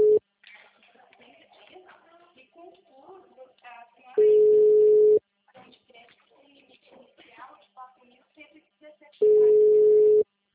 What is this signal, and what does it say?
Ringback tone (Europe, 425 Hz), with speech.
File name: out- -2010- - - .1174